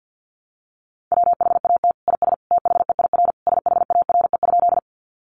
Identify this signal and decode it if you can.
Morse code, heard as a keyed tone, about 50 wpm, 700 Hz. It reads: WHAT IS THEIR SHARE?
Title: Sentences from Top 900 Words 50wpm